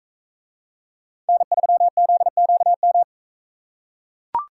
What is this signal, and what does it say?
Morse code, heard as a keyed tone, about 41 wpm, 700 Hz. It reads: N3ZQM